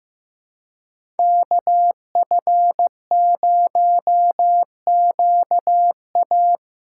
Morse code, heard as a keyed tone, about 15 wpm, 700 Hz. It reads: KF0QA
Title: USA Call Signs - 5 Characters - 3x Character Spacing - 15wpm